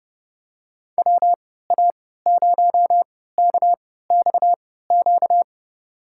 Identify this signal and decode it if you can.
Morse code, heard as a keyed tone, about 31 wpm, spaced as for 17 wpm, 700 Hz. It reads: WA0KXQ
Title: USA Call Signs - 3x Character Spacing - 30wpm